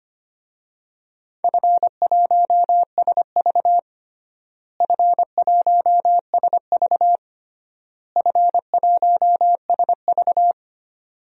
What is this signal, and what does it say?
Morse code, heard as a keyed tone, about 25 wpm, 700 Hz. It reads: F1SV F1SV F1SV
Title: DXCC-100 Call Signs - 4 Characters - Repeated 3x - 25wpm